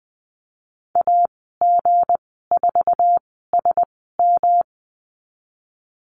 Morse code, heard as a keyed tone, about 20 wpm, spaced as for 15 wpm, 700 Hz. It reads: AG4SM
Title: 161 Lesson 26g - Letter g - Call Signs 20wpm